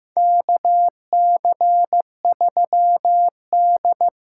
Morse code, heard as a keyed tone, about 15 wpm, 700 Hz. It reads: KC3D
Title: USA Call Signs - 6 Characters - 8x Character Spacing - 15wpm